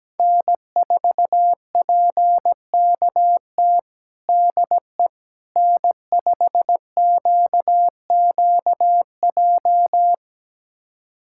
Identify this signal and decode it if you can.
Morse code, heard as a keyed tone, about 17 wpm, 700 Hz. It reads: N4PKT DE N5QQJ
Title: USA Callsign DE Callsign - 5 Characters - 17wpm